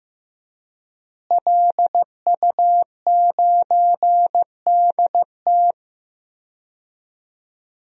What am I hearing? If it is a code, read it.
Morse code, heard as a keyed tone, about 15 wpm, 700 Hz. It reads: LU9DT
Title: DXCC-100 Call Signs - 5 Characters - Repeated 2x - 15wpm